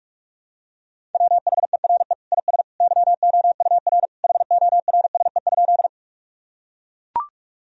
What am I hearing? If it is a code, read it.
Morse code, heard as a keyed tone, about 41 wpm, 700 Hz. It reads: WHERE IS YOUR HORSE?
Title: Sentences from Top 1000 Words - Review Part 1 - 45wpm